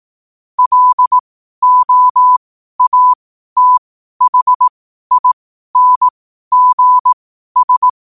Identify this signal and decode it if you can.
Morse code, heard as a keyed tone, about 18 wpm, spaced as for 12 wpm, 1000 Hz. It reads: LOATHINGS
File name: qotdfiles / 12wpm